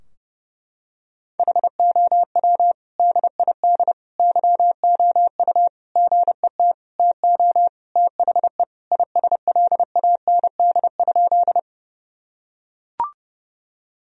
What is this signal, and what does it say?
Morse code, heard as a keyed tone, about 30 wpm, 700 Hz. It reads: HOW DID YOU GET TO THE ISLAND?
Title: Sentences from Top 1000 Words - Review Part 3 - 30wpm